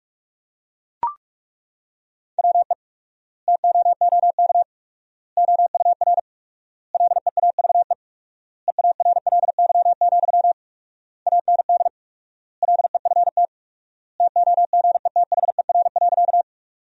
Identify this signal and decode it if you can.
Morse code, heard as a keyed tone, about 41 wpm, 700 Hz. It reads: WE TOOK OUR LEAVE EARLY, AND LEFT TOGETHER.
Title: 4x Word-Spacing - Sentences from Top 400 Words - 45wpm